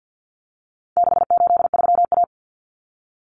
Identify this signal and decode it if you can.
Morse code, heard as a keyed tone, about 50 wpm, 700 Hz. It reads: 683A
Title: Common Cut Numbers - 4-Digit 50wpm